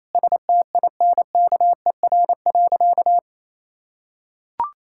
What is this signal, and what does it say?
Morse code, heard as a keyed tone, about 28 wpm, 700 Hz. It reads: STINKER.
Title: Pangram Challenge v2 - 28wpm